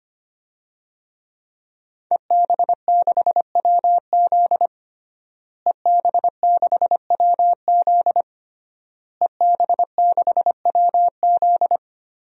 Morse code, heard as a keyed tone, about 25 wpm, 700 Hz. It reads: EB6WZ EB6WZ EB6WZ
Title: DXCC-100 Call Signs - 5 Characters - Repeated 3x - 25wpm